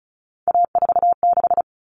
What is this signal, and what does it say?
Morse code, heard as a keyed tone, about 34 wpm, 700 Hz. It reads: A46
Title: Common Cut Numbers - 3-Digit 35wpm